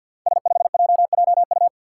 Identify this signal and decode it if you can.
Morse code, heard as a keyed tone, about 46 wpm, 700 Hz. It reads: S5JJU
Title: Five Challenging Letter-Number Combinations 50wpm